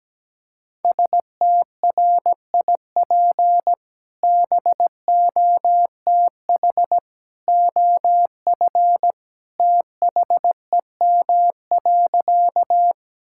Morse code, heard as a keyed tone, about 17 wpm, 700 Hz. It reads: STRIP BOTH OF THEM.
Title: Sentences from Top 1600 Words 17wpm